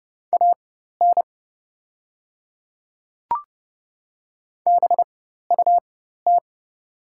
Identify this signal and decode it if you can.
Morse code, heard as a keyed tone, about 30 wpm, spaced as for 14 wpm, 700 Hz. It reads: AN BUT
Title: 4x Character Spacing - Top 100 Words - 30wpm